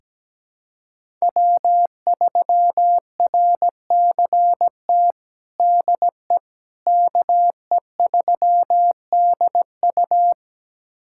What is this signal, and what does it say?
Morse code, heard as a keyed tone, about 17 wpm, 700 Hz. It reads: W3RCT DE KE3DU
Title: USA Callsign DE Callsign - 5 Characters - 17wpm